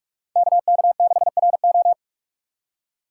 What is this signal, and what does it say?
Morse code, heard as a keyed tone, about 43 wpm, 700 Hz. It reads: KK6RO
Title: USA Call Signs - 5 Characters - 3x Character Spacing - 45wpm